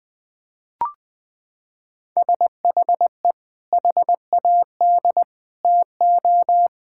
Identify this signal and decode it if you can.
Morse code, heard as a keyed tone, about 20 wpm, 700 Hz. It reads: SHE HAD TO
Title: Sentences with 4-Word Phrases - 20wpm